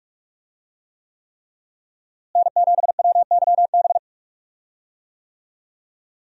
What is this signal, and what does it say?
Morse code, heard as a keyed tone, about 45 wpm, 700 Hz. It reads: N7WYB
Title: USA Call Signs - 5 Characters - Repeated 3x - 45wpm